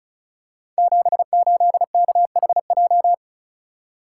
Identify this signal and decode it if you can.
Morse code, heard as a keyed tone, about 36 wpm, 700 Hz. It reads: 78KHJ
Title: Code Groups - 5 Letter:Numbers - 35wpm